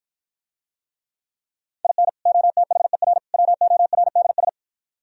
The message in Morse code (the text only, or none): IN OTHER WORDS